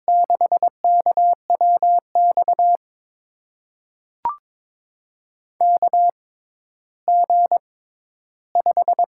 Morse code, heard as a keyed tone, about 22 wpm, spaced as for 13 wpm, 700 Hz. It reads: 6KWX KG5
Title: USA Call Signs - 6 Characters - 6x Character Spacing - 22wpm